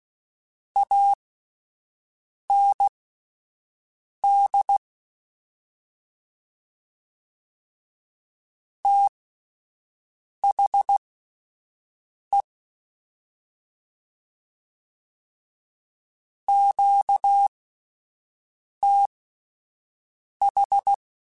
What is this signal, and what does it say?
Morse code, heard as a keyed tone, about 16 wpm, spaced as for 5 wpm, 800 Hz. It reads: AND THE QTH
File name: QSOs Spaced